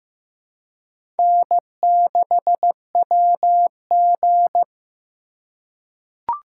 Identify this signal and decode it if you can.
Morse code, heard as a keyed tone, about 15 wpm, 700 Hz. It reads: N6WG
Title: K1USN SST QSO Practice - 15wpm